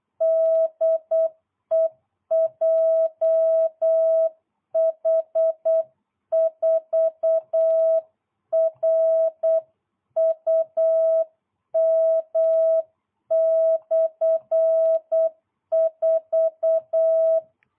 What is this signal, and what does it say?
Morse code, heard as a keyed tone, about 8 wpm, 660 Hz. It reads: DE JH4RUM/4